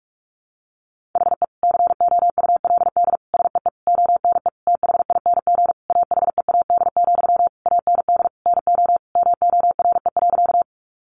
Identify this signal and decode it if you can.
Morse code, heard as a keyed tone, about 44 wpm, 700 Hz. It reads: HE COULD SEE ONE THING AHEAD, AND NO MORE.
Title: Sentences from Top 1100 Words 45wpm